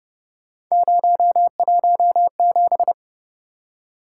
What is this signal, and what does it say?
Morse code, heard as a keyed tone, about 29 wpm, 700 Hz. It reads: 017